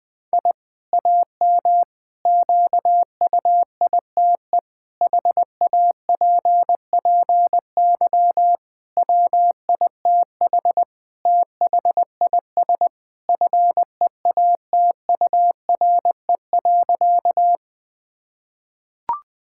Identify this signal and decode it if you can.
Morse code, heard as a keyed tone, about 20 wpm, 700 Hz. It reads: I AM QUITE HAPPY WITH THIS FEATURE.
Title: Sentences from Top 1000 Words - Review Part 5 - 20wpm